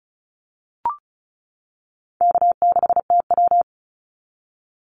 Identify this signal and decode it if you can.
Morse code, heard as a keyed tone, about 34 wpm, 700 Hz. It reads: K6TW
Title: NA SOTA Activator Callsigns 35wpm